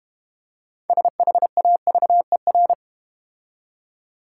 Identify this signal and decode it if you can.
Morse code, heard as a keyed tone, about 32 wpm, 690 Hz. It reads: SHAVER